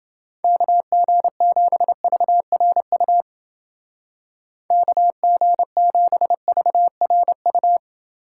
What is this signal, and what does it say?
Morse code, heard as a keyed tone, about 28 wpm, 700 Hz. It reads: KG7VRU KG7VRU